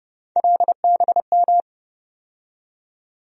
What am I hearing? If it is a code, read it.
Morse code, heard as a keyed tone, about 31 wpm, 700 Hz. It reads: LBM
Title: USA Call Sign Postfixes - 3 Letters 30wpm